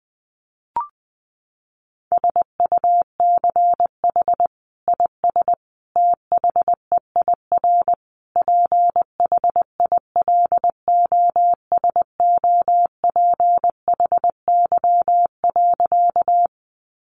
Morse code, heard as a keyed tone, about 20 wpm, 700 Hz. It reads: SUCH IS THEIR PHILOSOPHY.
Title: Sentences from Top 900 Words 20wpm